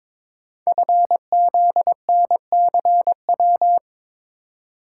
Morse code, heard as a keyed tone, about 22 wpm, 700 Hz. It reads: FZNCW
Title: Code Groups - 5 Letters - 22wpm